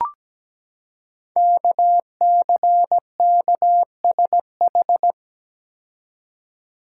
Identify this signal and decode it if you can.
Morse code, heard as a keyed tone, about 17 wpm, 700 Hz. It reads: KCKSH
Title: Code Groups - 5 Letters - 17wpm